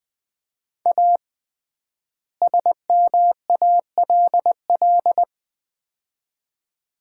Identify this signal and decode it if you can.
Morse code, heard as a keyed tone, about 20 wpm, 700 Hz. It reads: A SMALL
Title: 3x Word-Spacing - Sets of 2 Words - 20wpm